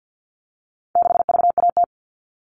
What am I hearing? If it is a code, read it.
Morse code, heard as a keyed tone, about 50 wpm, 700 Hz. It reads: 6VAT
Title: All Cut Numbers - 4-Digit 50wpm